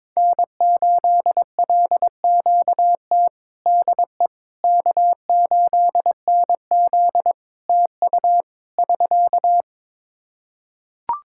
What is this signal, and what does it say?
Morse code, heard as a keyed tone, about 21 wpm, 700 Hz. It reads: N8LQT DE K8NZ TU <SK>
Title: Practice QSO - Exchange 4 - Ending - 22wpm